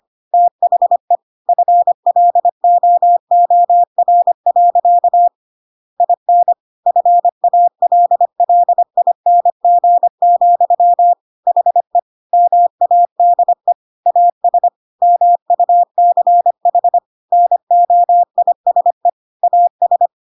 Morse code, heard as a keyed tone, about 25 wpm, 700 Hz. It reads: THE FLOOR. IN FALLING, HE MADE AS MUCH NOISE AS
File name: Pinocchio / Pinocchio ch7 25 25 ik1hge